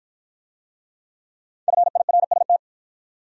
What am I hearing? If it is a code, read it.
Morse code, heard as a keyed tone, about 49 wpm, 700 Hz. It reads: FIRST